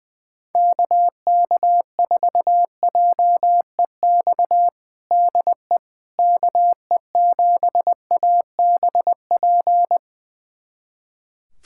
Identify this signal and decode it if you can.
Morse code, heard as a keyed tone, about 20 wpm, 700 Hz. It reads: KK4JEX DE KE7ABP
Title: USA Callsign DE Callsign - 6 Characters - 20wpm